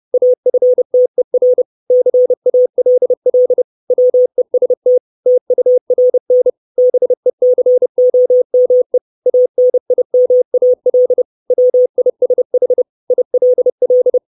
Morse code, heard as a keyed tone, about 30 wpm, 500 Hz. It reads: AFTER CALL WEST TURN BECOME ANIMAL WISH ILL
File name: 632 Words 30WPM